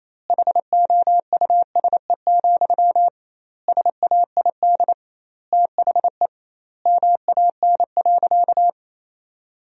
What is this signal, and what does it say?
Morse code, heard as a keyed tone, about 28 wpm, 700 Hz. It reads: HOUSE, SAID THE MAN.
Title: 2x Word-Spacing - Sentences from Top 400 Words - 28wpm